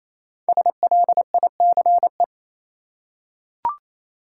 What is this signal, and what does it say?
Morse code, heard as a keyed tone, about 28 wpm, 700 Hz. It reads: SLICE
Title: Five Letter Words - 28wpm